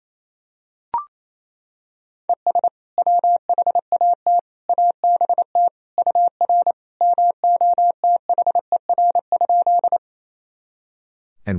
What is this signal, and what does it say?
Morse code, heard as a keyed tone, about 28 wpm, 700 Hz. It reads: ES WHAT ABT UR MOTHER?
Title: Sentences from Top 200 Words and Abbreviations 28wpm